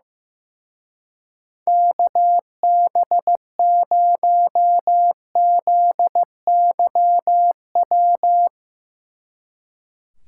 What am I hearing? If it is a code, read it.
Morse code, heard as a keyed tone, about 16 wpm, 700 Hz. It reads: KB0ZYW
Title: Speed Racing - USA 2x3 Call Signs 15wpm